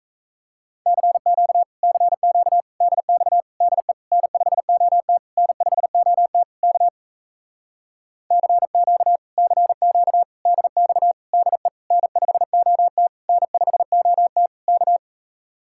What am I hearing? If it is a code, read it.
Morse code, heard as a keyed tone, about 38 wpm, 700 Hz. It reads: CQ CQ DX DE N5OT N5OT K CQ CQ DX DE N5OT N5OT K